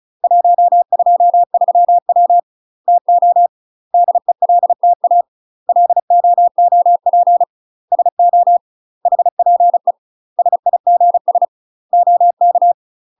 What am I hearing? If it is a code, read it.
Morse code, heard as a keyed tone, about 34 wpm, 700 Hz. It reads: 123W TO DELTA LOOP SO HPE SIGS OK
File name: UK MASTER QSOs TRAINING / 35WPM / UK MASTER QSOs 0015